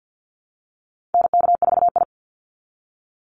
Morse code, heard as a keyed tone, about 49 wpm, 700 Hz. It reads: NK4I
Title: Call Signs Repeated 3x - 50wpm